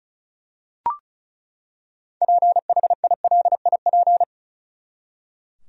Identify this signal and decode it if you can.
Morse code, heard as a keyed tone, about 36 wpm, 700 Hz. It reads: PHILIP
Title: USA Common Names - 35wpm